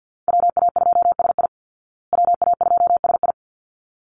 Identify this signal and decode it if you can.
Morse code, heard as a keyed tone, about 50 wpm, 700 Hz. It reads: WA2SI WA2SI
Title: Call Signs Repeated 3x - 50wpm